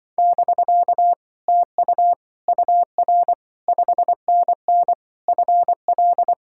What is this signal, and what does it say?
Morse code, heard as a keyed tone, about 25 wpm, 700 Hz. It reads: <BK> TU UR 5NN FL